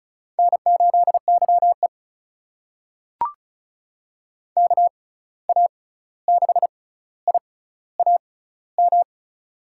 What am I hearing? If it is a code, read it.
Morse code, heard as a keyed tone, about 34 wpm, spaced as for 16 wpm, 700 Hz. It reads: N8YE KA6IAM
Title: USA Call Signs - 6x Character Spacing - 35wpm